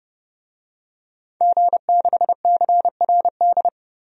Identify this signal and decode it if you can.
Morse code, heard as a keyed tone, about 31 wpm, 700 Hz. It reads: G6CRD
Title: DXCC-100 Call Signs - 5 Characters - 30wpm